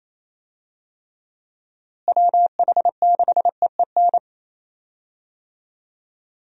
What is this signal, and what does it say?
Morse code, heard as a keyed tone, about 28 wpm, 700 Hz. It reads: WH6EEN
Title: USA Call Signs - 6 Characters - Repeated 2x - 28wpm